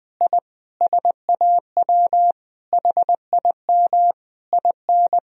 Morse code, heard as a keyed tone, about 21 wpm, 700 Hz. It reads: I SAW HIM IN
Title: Sentences from Top 1000 Words - Review Part 3 - 20wpm